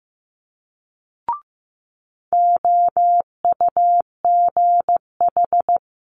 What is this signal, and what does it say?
Morse code, heard as a keyed tone, about 15 wpm, 700 Hz. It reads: OUGH
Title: Multi-letter Phonemes Quick Fire 15wpm